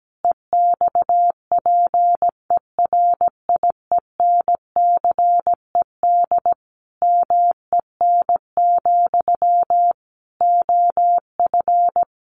Morse code, heard as a keyed tone, about 17 wpm, 700 Hz. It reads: EXPERIENCED MEN, OF
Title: Sentences from Top 1100 Words 17wpm